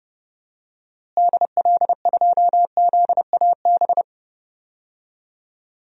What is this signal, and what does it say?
Morse code, heard as a keyed tone, about 30 wpm, 700 Hz. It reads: DL2ZAB
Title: DXCC-100 Call Signs - 6 Characters - Repeated 3x - 30wpm